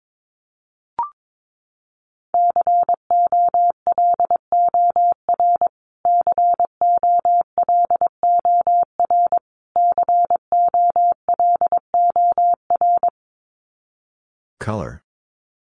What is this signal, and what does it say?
Morse code, heard as a keyed tone, about 22 wpm, 700 Hz. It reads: COLOR COLOR COLOR
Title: Top 300 Words Repeated 22wpm